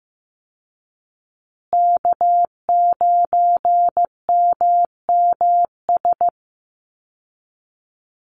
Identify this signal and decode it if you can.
Morse code, heard as a keyed tone, about 15 wpm, 700 Hz. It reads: K9MMS
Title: Call Signs Repeated 2x - 15wpm